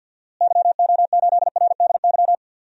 Morse code, heard as a keyed tone, about 51 wpm, 700 Hz. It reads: YO8RDY
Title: DXCC-100 Call Signs - 6 Characters - Repeated 3x - 50wpm